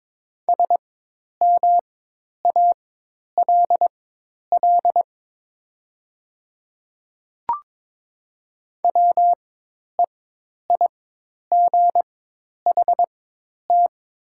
Morse code, heard as a keyed tone, about 22 wpm, spaced as for 10 wpm, 700 Hz. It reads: SMALL WEIGHT
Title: 4x Character Spacing - Top 500 Words - 22wpm